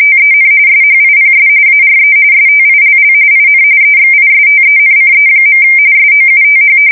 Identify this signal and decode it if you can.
RTTY, about 45.45 baud. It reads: TEN SHIPS ARE STILL IN EXISTENCE. SOUTH
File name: fsk170 / battleships